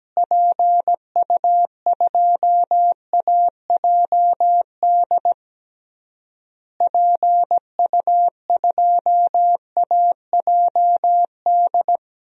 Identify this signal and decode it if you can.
Morse code, heard as a keyed tone, about 17 wpm, 700 Hz. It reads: PU2AJD PU2AJD
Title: DXCC-100 Call Signs - 6 Characters - Repeated 3x - 17wpm